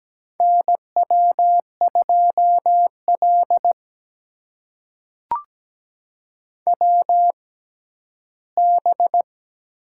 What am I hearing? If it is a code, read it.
Morse code, heard as a keyed tone, about 17 wpm, spaced as for 11 wpm, 700 Hz. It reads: NW2L WB
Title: USA Call Signs - 6x Character Spacing - 17wpm